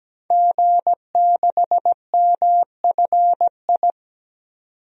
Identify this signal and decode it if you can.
Morse code, heard as a keyed tone, about 17 wpm, 700 Hz. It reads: G6MFI
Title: DXCC-100 Call Signs - 5 Characters - 17wpm